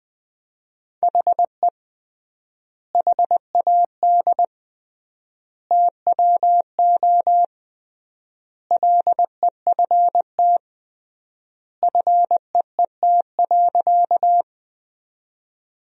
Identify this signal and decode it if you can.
Morse code, heard as a keyed tone, about 21 wpm, 700 Hz. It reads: HE HAD TWO LEFT FEET.
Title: 3x Word-Spacing - Sentences from Top 300 Words - 20wpm